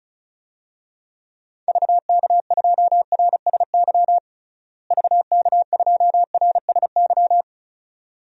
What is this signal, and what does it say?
Morse code, heard as a keyed tone, about 36 wpm, 700 Hz. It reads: VK2RSY VK2RSY
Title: DXCC-100 Call Signs - 6 Characters - Repeated 2x - 35wpm